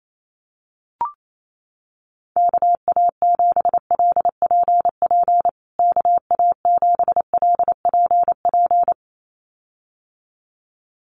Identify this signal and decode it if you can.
Morse code, heard as a keyed tone, about 28 wpm, 700 Hz. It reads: KA7LPP KA7LPP
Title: USA Call Signs Repeated 2x 28wpm